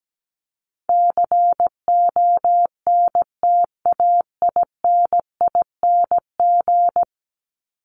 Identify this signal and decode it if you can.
Morse code, heard as a keyed tone, about 17 wpm, 700 Hz. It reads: CONTAINING
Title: ING Suffix - Top 100 Words with Base Words 17wpm